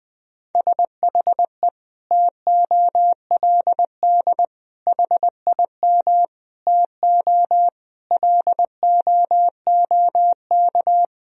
Morse code, heard as a keyed tone, about 20 wpm, 700 Hz. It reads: SHE TOLD HIM TO LOOK
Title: Sentences from Top 1000 Words - Review Part 1 - 20wpm